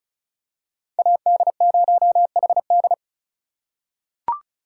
Morse code, heard as a keyed tone, about 36 wpm, 700 Hz. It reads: AD0HD
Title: USA Call Signs - 5 Characters - 4x Character Spacing - 35wpm